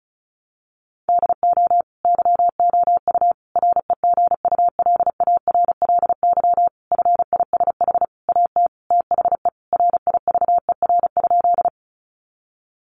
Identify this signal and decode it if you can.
Morse code, heard as a keyed tone, about 34 wpm, 700 Hz. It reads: DO YOU REGULARLY FISH AT THE RIVER?
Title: Sentences from Top 1400 Words 35wpm